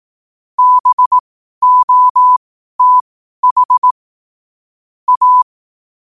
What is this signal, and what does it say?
Morse code, heard as a keyed tone, about 18 wpm, spaced as for 12 wpm, 1000 Hz. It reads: BOTH A
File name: qotdfiles / 12wpm